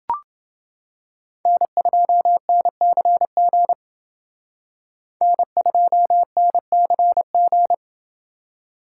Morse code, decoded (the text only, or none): N2NCG N2NCG